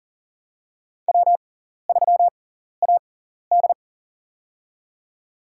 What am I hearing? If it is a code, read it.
Morse code, heard as a keyed tone, about 40 wpm, spaced as for 14 wpm, 700 Hz. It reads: W3AD
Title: USA Call Signs - 4 Characters - 6x Character Spacing - 40wpm